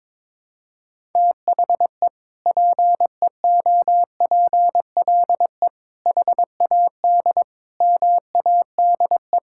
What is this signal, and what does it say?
Morse code, heard as a keyed tone, about 21 wpm, 700 Hz. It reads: THE PEOPLE HAD MADE
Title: Sentences from Top 1000 Words - Review Part 6 - 22wpm